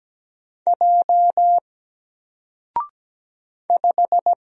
Morse code, heard as a keyed tone, about 17 wpm, 700 Hz. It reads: J 5
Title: Single Character (Letter:Number:Punctuation:Prosign) Alt 17wpm